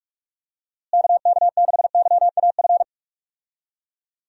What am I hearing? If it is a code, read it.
Morse code, heard as a keyed tone, about 43 wpm, 700 Hz. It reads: KK6YAF
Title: USA Call Signs - 6 Characters - Repeated 3x - 45wpm